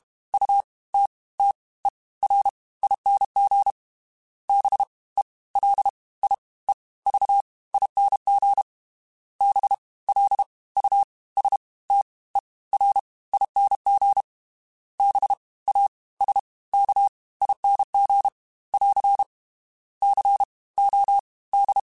Morse code, heard as a keyed tone, about 32 wpm, 800 Hz. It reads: U T T E R ING B E L I E V ING B L U S T E R ING B A S K ING <AR> C O D
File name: Special INGs / Special INGs-mp3s / Fast ing @30